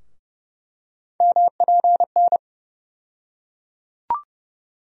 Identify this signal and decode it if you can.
Morse code, heard as a keyed tone, about 30 wpm, 700 Hz. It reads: MPN
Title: USA Call Sign Postfixes - 3 Letters 30wpm